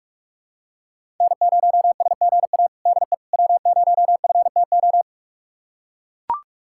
Morse code, heard as a keyed tone, about 44 wpm, 700 Hz. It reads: N0SGA DE W0FTO
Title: USA Callsign DE Callsign - 5 Characters - 45wpm